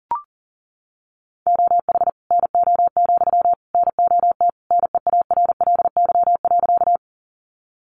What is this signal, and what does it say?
Morse code, heard as a keyed tone, about 40 wpm, 700 Hz. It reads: OH NO, NOT NEARLY.
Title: Sentences from Top 1300 Words 40wpm